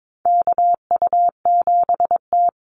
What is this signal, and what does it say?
Morse code, heard as a keyed tone, about 22 wpm, 700 Hz. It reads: KU7T
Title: Call Signs Repeated 3x - 22wpm